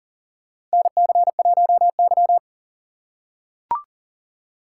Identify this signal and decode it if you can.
Morse code, heard as a keyed tone, about 38 wpm, 700 Hz. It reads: NC1Y
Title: USA Call Signs - 4 Characters - 4x Character Spacing - 40wpm